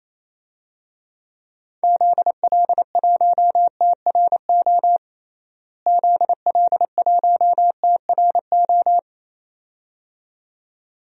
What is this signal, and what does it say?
Morse code, heard as a keyed tone, about 28 wpm, 700 Hz. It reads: ZL1TRO ZL1TRO
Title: DXCC-100 Call Signs - 6 Characters - Repeated 2x - 28wpm